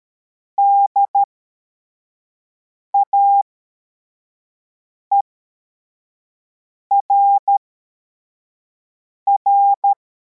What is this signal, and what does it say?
Morse code, heard as a keyed tone, about 13 wpm, spaced as for 4 wpm, 800 Hz. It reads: DAERR
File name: formation / cw / Text34